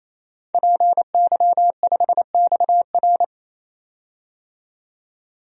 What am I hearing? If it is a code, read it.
Morse code, heard as a keyed tone, about 27 wpm, 700 Hz. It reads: PY5XR
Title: DXCC-100 Call Signs - 5 Characters - 28wpm